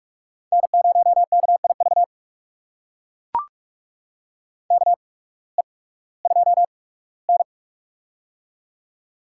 Morse code, heard as a keyed tone, about 44 wpm, spaced as for 21 wpm, 700 Hz. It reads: N0KIV KE2N
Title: USA Call Signs - 8x Character Spacing - 45wpm